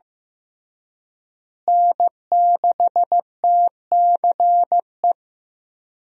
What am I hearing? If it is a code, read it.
Morse code, heard as a keyed tone, about 15 wpm, 700 Hz. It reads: N6TCE
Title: USA Call Signs - 5 Characters - Repeated 3x - 15wpm